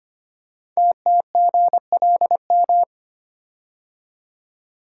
Morse code, heard as a keyed tone, about 25 wpm, 690 Hz. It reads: TTGLM
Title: Code Groups - 5 Letters - 25wpm